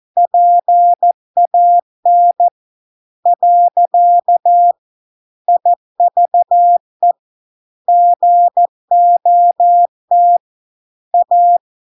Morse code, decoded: PAN . IVE GOT A

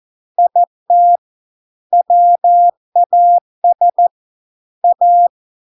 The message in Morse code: IT WAS A

